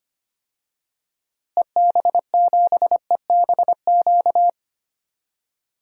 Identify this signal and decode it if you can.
Morse code, heard as a keyed tone, about 25 wpm, 700 Hz. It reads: EB7EBQ
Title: DXCC-100 Call Signs - 6 Characters - Repeated 3x - 25wpm